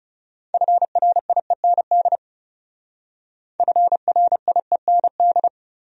Morse code, heard as a keyed tone, about 38 wpm, 700 Hz. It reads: FRIEND FRIEND
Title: Speed-Racing Top 300 Words 30wpm